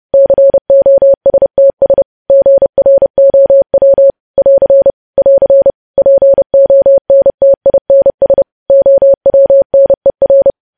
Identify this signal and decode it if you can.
Morse code, heard as a keyed tone, about 30 wpm, 550 Hz. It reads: COSTS GROW <AR> <AR> PONTINS OWNER
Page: news headlines in morse code at 30 wpm